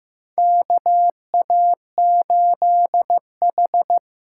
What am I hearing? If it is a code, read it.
Morse code, heard as a keyed tone, about 15 wpm, 700 Hz. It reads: KA8H
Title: USA Call Signs - 6 Characters - 8x Character Spacing - 15wpm